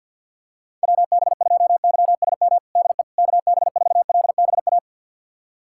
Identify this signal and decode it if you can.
Morse code, heard as a keyed tone, about 52 wpm, 700 Hz. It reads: WB2YSM DE KB4LBA
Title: USA Callsign DE Callsign 50wpm